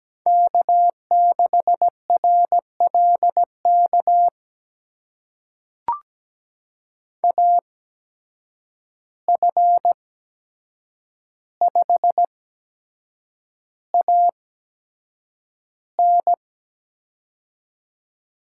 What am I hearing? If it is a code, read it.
Morse code, heard as a keyed tone, about 17 wpm, spaced as for 7 wpm, 700 Hz. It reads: K6RLK AF5AN
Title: USA Call Signs - 5 Characters - 8x Character Spacing - 17wpm